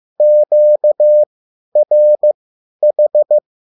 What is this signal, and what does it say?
Morse code, heard as a keyed tone, about 15 wpm, spaced as for 10 wpm, 600 Hz. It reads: QRH
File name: QSL / CW lesson